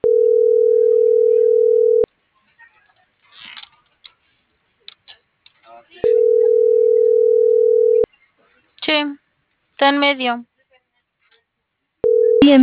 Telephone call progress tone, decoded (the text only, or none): ringback tone